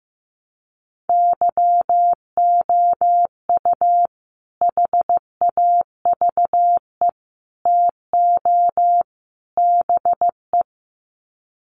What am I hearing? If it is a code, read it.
Morse code, heard as a keyed tone, about 15 wpm, 700 Hz. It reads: YOU HAVE TO BE
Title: Sets of 4 Words 15wpm